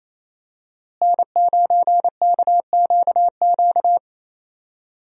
Morse code, heard as a keyed tone, about 27 wpm, 700 Hz. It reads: N9KQQ
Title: USA Call Signs - 5 Characters - Repeated 2x - 28wpm